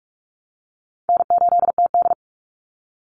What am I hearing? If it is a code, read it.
Morse code, heard as a keyed tone, about 44 wpm, 700 Hz. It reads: N8TD
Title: Easy Call Signs Repeated 2x - 45wpm